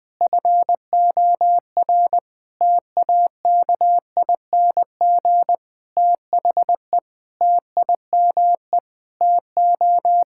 Morse code, heard as a keyed tone, about 20 wpm, 700 Hz. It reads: FOR TAKING THE TIME TO
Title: Sentences with 7-Word Phrases - Spoken Beginning - 20wpm